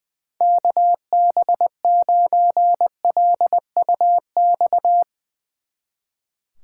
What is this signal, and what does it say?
Morse code, heard as a keyed tone, about 22 wpm, 700 Hz. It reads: KB9LUX
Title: Speed Racing - USA 2x3 Call Signs 20wpm